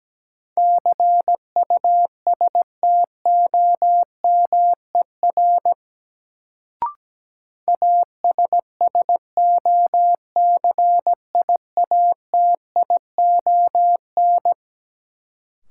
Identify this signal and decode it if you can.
Morse code, heard as a keyed tone, about 17 wpm, 700 Hz. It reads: CUSTOMER ASSOCIATION
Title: Sentences from Top 1100 Words - New Word Drill - 17wpm